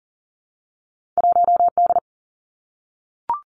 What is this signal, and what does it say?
Morse code, heard as a keyed tone, about 39 wpm, 700 Hz. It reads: 1D
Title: Two Challenging Letter-Number Combinations 40wpm